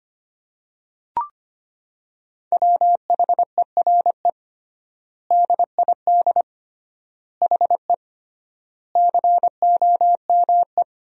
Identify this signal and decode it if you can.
Morse code, heard as a keyed tone, about 25 wpm, 700 Hz. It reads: WHERE DID HE COME
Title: 3x Word-Spacing - Sentences from Top 200 Words - 25wpm